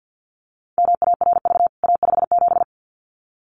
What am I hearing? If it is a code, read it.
Morse code, heard as a keyed tone, about 49 wpm, 700 Hz. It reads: MARV A57
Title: CWT Exchange 50wpm